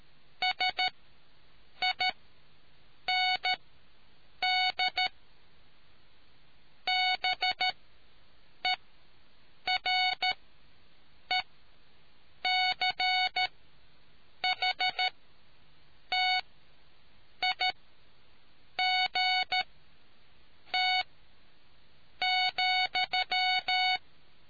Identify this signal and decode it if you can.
Morse code, heard as a keyed tone, about 13 wpm, spaced as for 7 wpm, 740 Hz. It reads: SIND BERECHTIGT ,